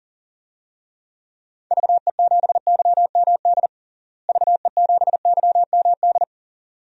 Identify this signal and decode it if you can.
Morse code, heard as a keyed tone, about 40 wpm, 700 Hz. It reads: VE7YMD VE7YMD
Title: DXCC-100 Call Signs - 6 Characters - Repeated 2x - 40wpm